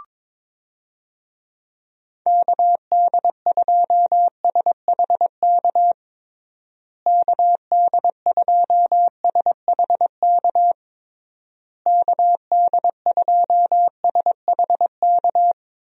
Morse code, heard as a keyed tone, about 22 wpm, 700 Hz. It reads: KD2SHK KD2SHK KD2SHK
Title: USA Call Signs - 6 Characters - Repeated 3x - 22wpm